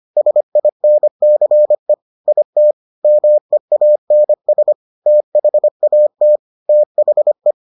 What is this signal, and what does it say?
Morse code, heard as a keyed tone, about 25 wpm, 600 Hz. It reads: SINCE IT MEANS THAT THE